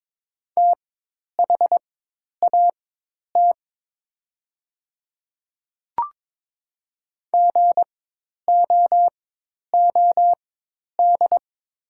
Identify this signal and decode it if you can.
Morse code, heard as a keyed tone, about 22 wpm, spaced as for 10 wpm, 690 Hz. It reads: THAT GOOD
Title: 4x Character Spacing - Top 300 Words - 22wpm